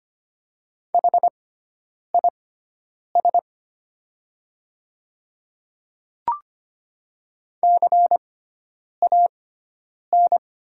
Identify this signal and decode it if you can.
Morse code, heard as a keyed tone, about 25 wpm, spaced as for 9 wpm, 700 Hz. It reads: HIS CAN
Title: 6x Character Spacing - Top 100 Words - 25wpm